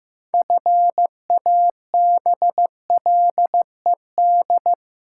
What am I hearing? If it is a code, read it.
Morse code, heard as a keyed tone, about 15 wpm, 700 Hz. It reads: FABLED
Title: Pangram Challenge v2 - 15wpm